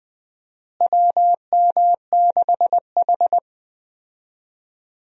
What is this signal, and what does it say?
Morse code, heard as a keyed tone, about 20 wpm, 700 Hz. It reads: WM6H
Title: USA Call Signs - 4 Characters - Repeated 2x - 20wpm